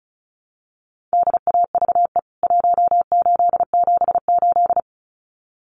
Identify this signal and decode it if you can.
Morse code, heard as a keyed tone, about 34 wpm, 700 Hz. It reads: DAVE 1878
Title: CWT Exchange 35wpm